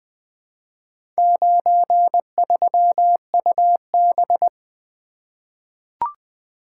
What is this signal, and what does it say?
Morse code, heard as a keyed tone, about 20 wpm, 700 Hz. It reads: 93UB
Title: Four Letter-Number Combinations - 20wpm